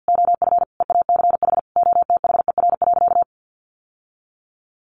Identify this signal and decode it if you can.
Morse code, heard as a keyed tone, about 48 wpm, 700 Hz. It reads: OF EACH OTHER.
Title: Sentences from Top 700 Words 50wpm